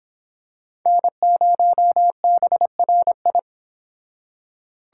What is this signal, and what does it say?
Morse code, heard as a keyed tone, about 25 wpm, 700 Hz. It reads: N0BRI